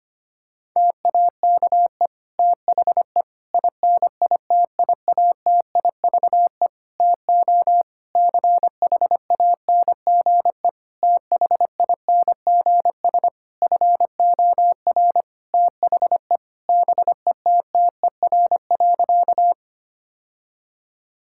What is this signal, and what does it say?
Morse code, heard as a keyed tone, about 25 wpm, 700 Hz. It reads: TAKE THE INITIATIVE TO CHANGE THINGS FOR THE BETTER.
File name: practice / 25wpm / Difficult Sentences with Hints - 25wpm